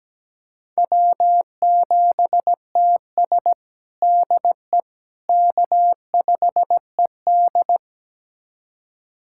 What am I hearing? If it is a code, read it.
Morse code, heard as a keyed tone, about 17 wpm, 700 Hz. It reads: W7TS DE K5ED
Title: USA Callsign DE Callsign - 4 Characters - 17wpm